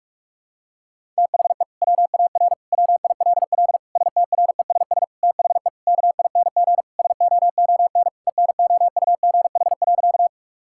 Morse code, heard as a keyed tone, about 47 wpm, 700 Hz. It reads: THE WAR WILL STRESS THE KING SOON ENOUGH.
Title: Sentences from Top 1000 Words - Review Part 6 - 45wpm